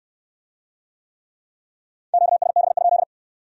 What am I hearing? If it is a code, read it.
Morse code, heard as a keyed tone, about 65 wpm, 700 Hz. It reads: QSB?